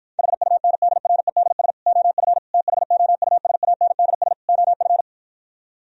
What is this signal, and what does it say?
Morse code, heard as a keyed tone, about 53 wpm, 700 Hz. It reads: HUNDREDS OF THOUSANDS OF